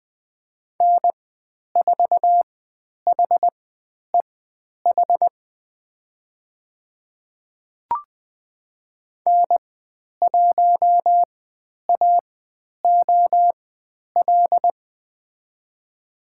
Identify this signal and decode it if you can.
Morse code, heard as a keyed tone, about 20 wpm, spaced as for 10 wpm, 700 Hz. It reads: N4HEH N1AOL
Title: 102 Lesson 17g - Letter u - Call Signs 20wpm